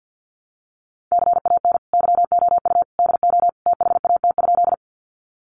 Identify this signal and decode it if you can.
Morse code, heard as a keyed tone, about 50 wpm, 700 Hz. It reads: CAN YOU DO THAT?
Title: Sentences from Top 100 Words 50wpm